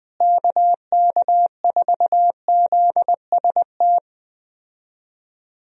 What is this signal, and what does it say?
Morse code, heard as a keyed tone, about 20 wpm, 700 Hz. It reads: KK4ZST
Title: USA Call Signs - 6 Characters - Repeated 3x - 20wpm